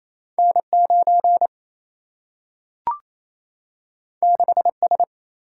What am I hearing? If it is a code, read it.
Morse code, heard as a keyed tone, about 28 wpm, 700 Hz. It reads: N 9 6 S
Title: Single-Letter Single-Number 28wpm